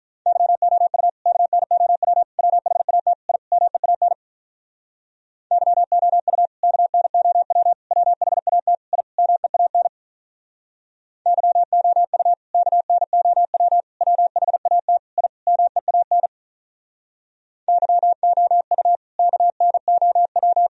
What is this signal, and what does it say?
Morse code, heard as a keyed tone, about 46 wpm, 700 Hz. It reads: YOU KNOW WHAT I MEAN YOU KNOW WHAT I MEAN YOU KNOW WHAT I MEAN YOU KNOW